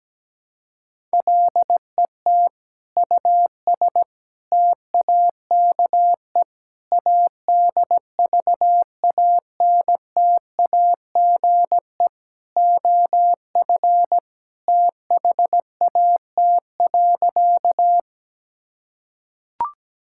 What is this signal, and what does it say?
Morse code, heard as a keyed tone, about 17 wpm, 700 Hz. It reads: LET US TAKE ADVANTAGE OF THAT.
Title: Sentences from Top 1000 Words - Review Part 6 - 17wpm